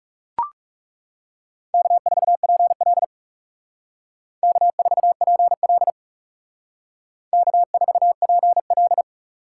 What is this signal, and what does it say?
Morse code, heard as a keyed tone, about 36 wpm, 700 Hz. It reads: K4PL K4PL K4PL
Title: Speed Racing - USA Call Signs 30wpm